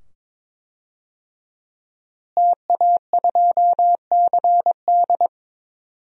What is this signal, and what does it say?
Morse code, heard as a keyed tone, about 22 wpm, 700 Hz. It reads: TA2CD
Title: DXCC-100 Call Signs - 5 Characters - Repeated 2x - 22wpm